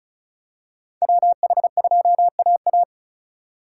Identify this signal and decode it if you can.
Morse code, heard as a keyed tone, about 36 wpm, 700 Hz. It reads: WH2AA
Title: USA Call Signs - 5 Characters - 4x Character Spacing - 35wpm